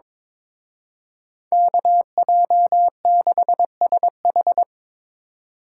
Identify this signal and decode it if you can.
Morse code, heard as a keyed tone, about 25 wpm, 700 Hz. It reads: KJ6SH
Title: Speed Racing - USA 2x2 Call Signs 22wpm